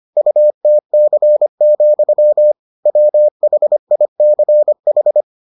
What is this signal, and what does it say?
Morse code, heard as a keyed tone, about 25 wpm, 600 Hz. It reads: UTC, WHICH